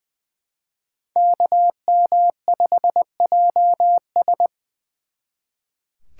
Morse code, heard as a keyed tone, about 23 wpm, 700 Hz. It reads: KM5JS
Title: Speed Racing - USA Call Signs 20wpm